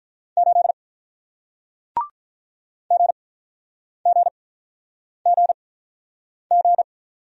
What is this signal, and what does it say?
Morse code, heard as a keyed tone, about 48 wpm, spaced as for 9 wpm, 700 Hz. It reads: 8 GGGG